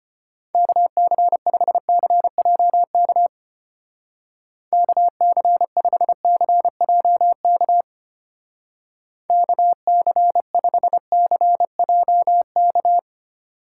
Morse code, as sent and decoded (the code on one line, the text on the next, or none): -.- -.-. ..... -.-. .--- -.- / -.- -.-. ..... -.-. .--- -.- / -.- -.-. ..... -.-. .--- -.-
KC5CJK KC5CJK KC5CJK